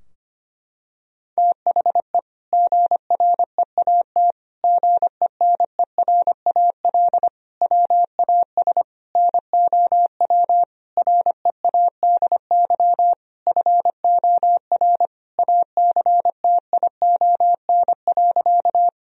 Morse code, heard as a keyed tone, about 25 wpm, 700 Hz. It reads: THE GREAT GENERAL WAS NOW READY FOR ACTION.
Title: Sentences from Top 1000 Words - Review Part 4 - 25wpm